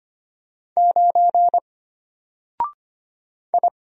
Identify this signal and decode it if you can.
Morse code, heard as a keyed tone, about 25 wpm, 700 Hz. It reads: 9 I
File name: Single-Letter-Number-25wpm